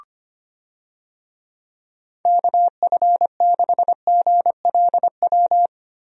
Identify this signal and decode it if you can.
Morse code, heard as a keyed tone, about 25 wpm, 700 Hz. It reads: KF6GLW
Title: USA Call Signs - 6 Characters - 25wpm